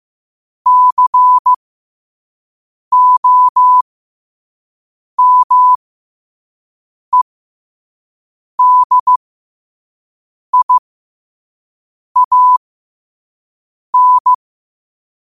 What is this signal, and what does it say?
Morse code, heard as a keyed tone, about 15 wpm, spaced as for 5 wpm, 1000 Hz. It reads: COMEDIAN
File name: qotdfiles / 5wpm